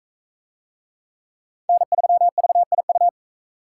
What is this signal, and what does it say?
Morse code, heard as a keyed tone, about 39 wpm, 700 Hz. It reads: N3VIU